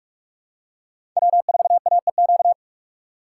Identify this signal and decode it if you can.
Morse code, heard as a keyed tone, about 43 wpm, 700 Hz. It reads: W4AEQ
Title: USA Call Signs - 3x Character Spacing - 45wpm